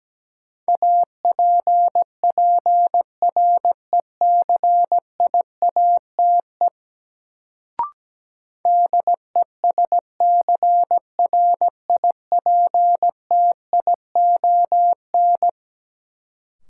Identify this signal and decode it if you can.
Morse code, heard as a keyed tone, about 17 wpm, 700 Hz. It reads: APPRECIATE DESCRIPTION
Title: Sentences from Top 1300 Words - New Word Drill Alt - 17wpm